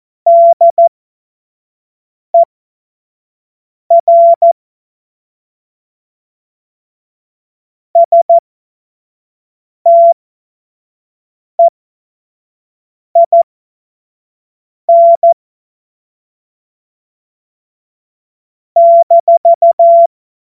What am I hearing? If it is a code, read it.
Morse code, heard as a keyed tone, about 14 wpm, spaced as for 5 wpm, 680 Hz. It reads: DER STEIN -
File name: Morsekurs / Witze / Witz 01. August 2023 25 F70